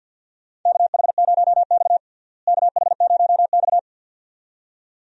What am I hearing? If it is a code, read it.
Morse code, heard as a keyed tone, about 46 wpm, 700 Hz. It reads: KH0X KH0X
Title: USA Call Signs - 4 Characters - Repeated 2x - 50wpm